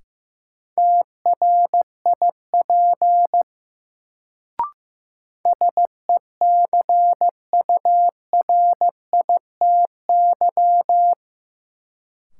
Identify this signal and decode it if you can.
Morse code, heard as a keyed tone, about 16 wpm, 700 Hz. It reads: TRIP SECURITY
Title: Sentences from Top 1000 Words - Review Part 4 - New Word Drill - 15wpm